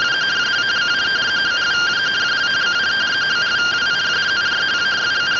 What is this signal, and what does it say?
Two-tone teleprinter signal, about 170 Hz shift.